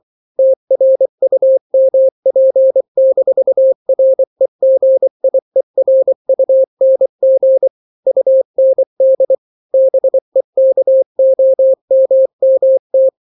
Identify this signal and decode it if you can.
Morse code, heard as a keyed tone, about 24 wpm, 530 Hz. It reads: TRUMP-REGIERUNG UND BEKOMMT
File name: daily / 23wpm